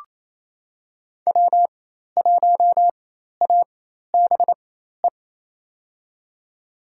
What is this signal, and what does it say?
Morse code, heard as a keyed tone, about 28 wpm, spaced as for 13 wpm, 700 Hz. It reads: W1ABE
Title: USA Call Signs - 4x Character Spacing - 28wpm